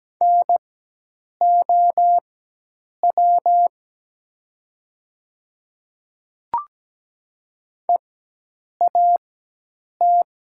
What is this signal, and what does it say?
Morse code, heard as a keyed tone, about 17 wpm, spaced as for 8 wpm, 700 Hz. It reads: NOW EAT
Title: 4x Character Spacing - Top 300 Words - 17wpm